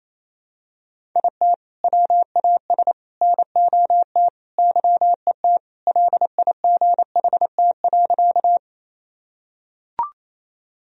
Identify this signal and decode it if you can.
Morse code, heard as a keyed tone, about 28 wpm, 700 Hz. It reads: IT WAS NOT YET LIGHT.
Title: Sentences from Top 1000 Words - Review Part 1 - 28wpm